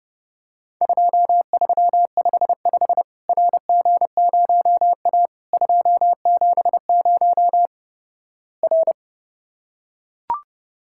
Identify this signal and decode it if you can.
Morse code, heard as a keyed tone, about 29 wpm, 700 Hz. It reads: 2355 RG0A 270 R
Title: Worked All Europe DX Contest - QTC Exchange - 30wpm